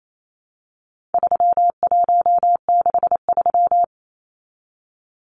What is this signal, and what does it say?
Morse code, heard as a keyed tone, about 28 wpm, 700 Hz. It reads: 3163
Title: Numbers - 4-Digit 28wpm